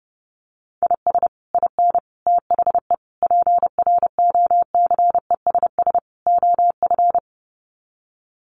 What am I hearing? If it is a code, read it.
Morse code, heard as a keyed tone, about 30 wpm, 700 Hz. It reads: IS IN THE PROCESS OF
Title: Sets of 5 Words Encore2 30wpm